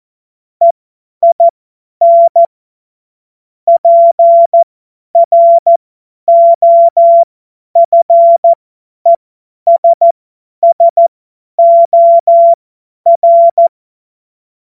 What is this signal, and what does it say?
Morse code, heard as a keyed tone, about 14 wpm, spaced as for 10 wpm, 680 Hz. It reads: EIN PROFESSOR
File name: Morsekurs / Witze / Witz 21. Juni 2022 50 F70